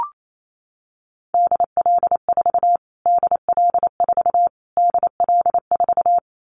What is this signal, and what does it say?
Morse code, heard as a keyed tone, about 28 wpm, 700 Hz. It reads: DL4 DL4 DL4
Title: Callsign Prefixes for Top 25 DXCC - Repeated 3x - 28wpm